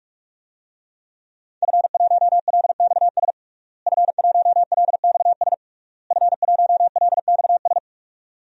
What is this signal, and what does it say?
Morse code, heard as a keyed tone, about 42 wpm, 700 Hz. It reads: F1LXS F1LXS F1LXS
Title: DXCC-100 Call Signs - 5 Characters - Repeated 3x - 45wpm